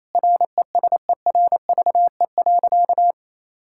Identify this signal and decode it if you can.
Morse code, heard as a keyed tone, about 28 wpm, 700 Hz. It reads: RESERVE.
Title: Sentences from Top 1000 Words - Review Part 6 - 28wpm